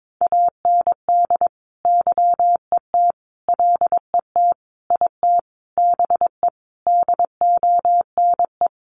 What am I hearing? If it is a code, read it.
Morse code, heard as a keyed tone, about 22 wpm, 700 Hz. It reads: AND YET LET IT BE DONE
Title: Sentences from Top 1200 Words 22wpm